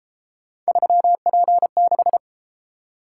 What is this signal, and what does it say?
Morse code, heard as a keyed tone, about 32 wpm, 700 Hz. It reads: 3P6